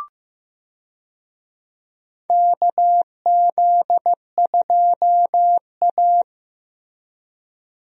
Morse code, heard as a keyed tone, about 15 wpm, 700 Hz. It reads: KZ2A
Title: USA Call Signs - 4 Characters - Repeated 2x - 15wpm